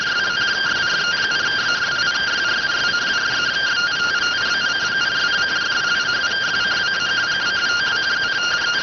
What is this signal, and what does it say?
Two-tone teleprinter signal, about 170 Hz shift.